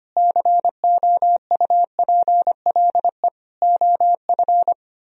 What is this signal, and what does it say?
Morse code, heard as a keyed tone, about 33 wpm, 700 Hz. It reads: COUPLE OF